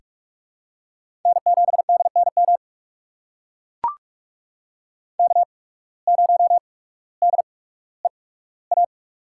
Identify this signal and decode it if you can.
Morse code, heard as a keyed tone, about 44 wpm, spaced as for 19 wpm, 700 Hz. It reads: N7DNM K0DEA
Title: USA Call Signs - 5 Characters - 8x Character Spacing - 45wpm